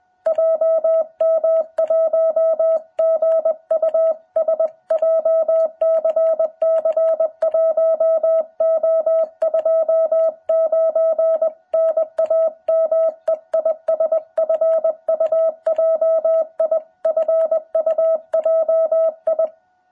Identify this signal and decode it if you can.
Morse code, heard as a keyed tone, about 20 wpm, 660 Hz. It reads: JM1GUS JCC1O29 NAME IS FUJI FUJI
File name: jh8cbh / 2021cw